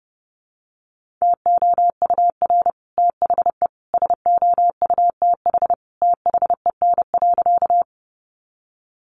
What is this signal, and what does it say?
Morse code, heard as a keyed tone, about 30 wpm, 700 Hz. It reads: TOUR THE SOUTH THEN.
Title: Sentences from Top 1400 Words 30wpm